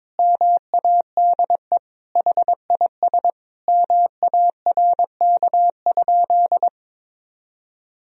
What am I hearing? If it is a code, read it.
Morse code, heard as a keyed tone, about 22 wpm, 700 Hz. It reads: MADE HIS MARK?
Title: Sentences from Top 300 Words and Abbreviations 22wpm